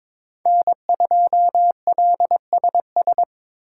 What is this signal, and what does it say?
Morse code, heard as a keyed tone, about 25 wpm, 700 Hz. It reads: N2LSS
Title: Speed Racing - USA 1x3 Call Signs 22wpm